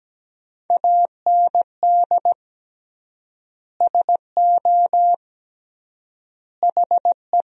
Morse code, heard as a keyed tone, about 17 wpm, 700 Hz. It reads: AND SO HE
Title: 3x Word-Spacing - Sentences from Top 500 Words - 17wpm